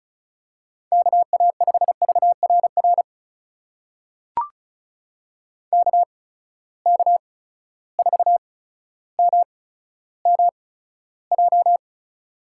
Morse code, heard as a keyed tone, about 36 wpm, spaced as for 15 wpm, 700 Hz. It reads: KA5VRR KK4MMJ
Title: USA Call Signs - 6 Characters - 8x Character Spacing - 35wpm